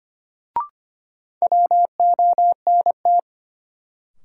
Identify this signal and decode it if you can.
Morse code, heard as a keyed tone, about 25 wpm, 700 Hz. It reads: WONT
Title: Sentences from Top 1000 Words - Review Part 6 - New Word Drill - 25wpm